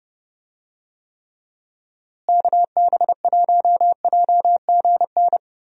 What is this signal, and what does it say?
Morse code, heard as a keyed tone, about 31 wpm, 700 Hz. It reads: KB1JGN
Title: USA Call Signs - 6 Characters - Repeated 3x - 30wpm